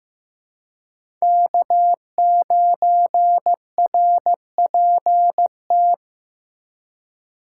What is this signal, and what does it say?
Morse code, heard as a keyed tone, about 16 wpm, 700 Hz. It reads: K9RPT
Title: USA Call Signs - 5 Characters - 4x Character Spacing - 15wpm